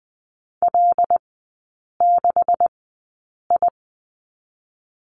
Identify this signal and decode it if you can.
Morse code, heard as a keyed tone, about 20 wpm, spaced as for 8 wpm, 700 Hz. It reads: L6I
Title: 127 Lesson 21e - Number 6 - Three Characters 20wpm